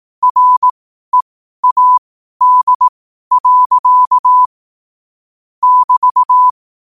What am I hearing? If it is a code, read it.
Morse code, heard as a keyed tone, about 18 wpm, spaced as for 12 wpm, 1000 Hz. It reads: READ. <BT>
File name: qotdfiles / 12wpm